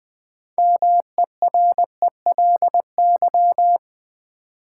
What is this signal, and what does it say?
Morse code, heard as a keyed tone, about 20 wpm, 700 Hz. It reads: MERELY
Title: Build A Word - 3 Letters At A Time - 6-Letter Words - 20wpm